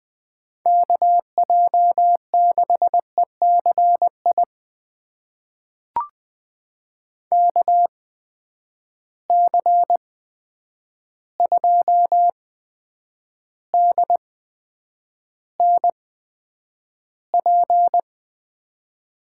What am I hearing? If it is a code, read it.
Morse code, heard as a keyed tone, about 20 wpm, spaced as for 9 wpm, 700 Hz. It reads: KJ6ECI KC2DNP
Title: USA Call Signs - 6 Characters - 8x Character Spacing - 20wpm